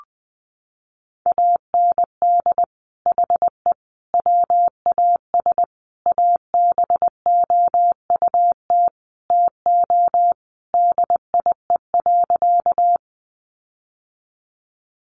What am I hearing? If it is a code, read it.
Morse code, heard as a keyed tone, about 20 wpm, 700 Hz. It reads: AND HE WAS ABOUT TO DIE.
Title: Sentences from Top 1300 Words 20wpm